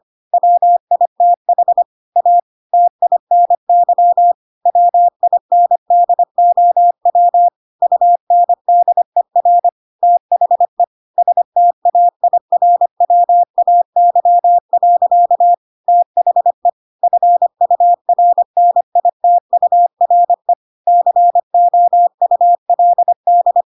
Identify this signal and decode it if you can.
Morse code, heard as a keyed tone, about 25 wpm, 700 Hz. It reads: WITH A TINY WINDOW UNDER THE STAIRWAY. THE FURNITURE COULD
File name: Pinocchio / Pinocchio ch3 25 25 ik1hge